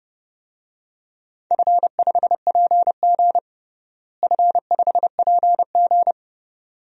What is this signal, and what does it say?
Morse code, heard as a keyed tone, about 30 wpm, 700 Hz. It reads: F5PG F5PG
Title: DXCC-100 Call Signs - 4 Characters - Repeated 2x - 30wpm